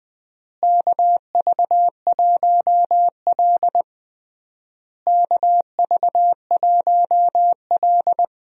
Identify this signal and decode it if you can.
Morse code, heard as a keyed tone, about 21 wpm, 700 Hz. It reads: KV1L KV1L
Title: USA Call Signs - 4 Characters - Repeated 3x - 20wpm